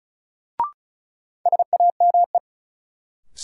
Morse code, heard as a keyed tone, about 36 wpm, 700 Hz. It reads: SAME
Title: Sentences from Top 200 Words - New Word Drill - 35wpm